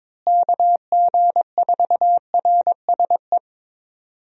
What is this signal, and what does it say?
Morse code, heard as a keyed tone, about 22 wpm, 700 Hz. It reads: KG4RSE
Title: USA Call Signs - 6 Characters - Repeated 2x - 22wpm